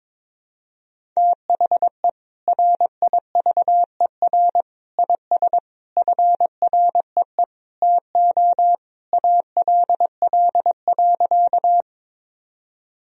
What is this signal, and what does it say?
Morse code, heard as a keyed tone, about 22 wpm, 700 Hz. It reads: THE RIVER IS FREE TO ALL.
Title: Sentences from Top 1000 Words - Review Part 1 - 22wpm